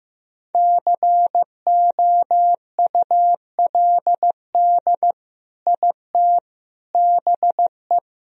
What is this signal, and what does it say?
Morse code, heard as a keyed tone, about 16 wpm, 700 Hz. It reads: COULD IT BE
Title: Sentences from Top 1000 Words - Review Part 1 - 15wpm